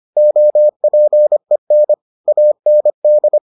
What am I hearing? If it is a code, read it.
Morse code, heard as a keyed tone, about 25 wpm, 600 Hz. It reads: OPEN AND